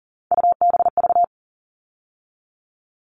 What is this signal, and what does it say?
Morse code, heard as a keyed tone, about 40 wpm, 700 Hz. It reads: UBV
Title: All Cut Numbers - 3-Digit 40wpm